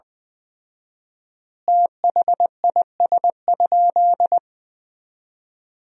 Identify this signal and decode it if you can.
Morse code, heard as a keyed tone, about 20 wpm, 700 Hz. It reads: THIS?
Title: 4x Word-Spacing - Sentences from Top 200 Words - 20wpm